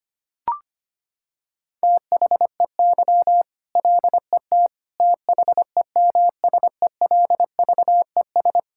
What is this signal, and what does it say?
Morse code, heard as a keyed tone, about 25 wpm, 700 Hz. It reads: THEY LET THEMSELVES
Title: Sentences from Top 1000 Words - Review Part 5 - 25wpm